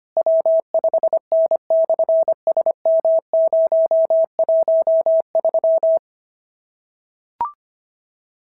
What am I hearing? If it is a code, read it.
Morse code, heard as a keyed tone, about 22 wpm, 650 Hz. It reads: W5N/SM013
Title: SOTA QSO Practice 25wpm